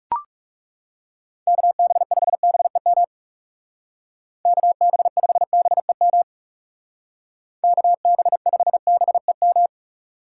Morse code, heard as a keyed tone, about 36 wpm, 700 Hz. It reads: KB5BEM KB5BEM KB5BEM
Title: Speed Racing - USA 2x3 Call Signs 30wpm